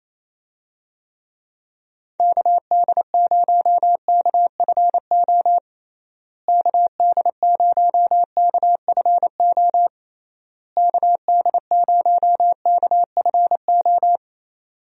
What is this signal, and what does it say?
Morse code, heard as a keyed tone, about 28 wpm, 700 Hz. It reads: KD0KFO KD0KFO KD0KFO
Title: USA Call Signs - 6 Characters - Repeated 3x - 28wpm